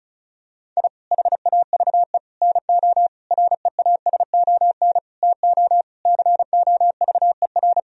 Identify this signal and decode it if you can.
Morse code, heard as a keyed tone, about 36 wpm, 700 Hz. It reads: I HAVE NO REASON TO COVER
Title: Sentences from Top 1000 Words - Review Part 1 - 35wpm